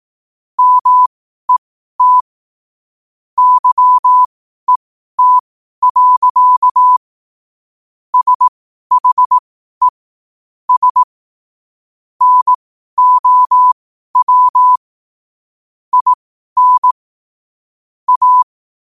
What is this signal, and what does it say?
Morse code, heard as a keyed tone, about 18 wpm, spaced as for 12 wpm, 1000 Hz. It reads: MET YET. SHE S NOW IN A
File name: qotdfiles / 12wpm